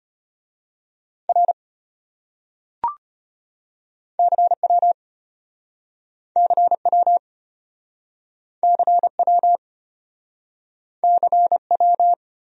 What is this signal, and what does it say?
Morse code, heard as a keyed tone, about 33 wpm, 700 Hz. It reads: R CW CW CW CW